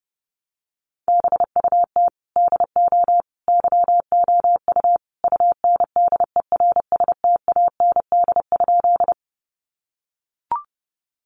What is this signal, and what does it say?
Morse code, heard as a keyed tone, about 30 wpm, 700 Hz. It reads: BUT DO YOU UNDERSTAND?
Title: Sentences from Top 600 Words 30wpm